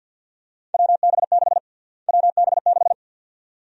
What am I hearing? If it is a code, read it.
Morse code, heard as a keyed tone, about 49 wpm, 700 Hz. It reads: WB6 WB6
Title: USA Call Sign Prefixes - Repeated 2x - 50wpm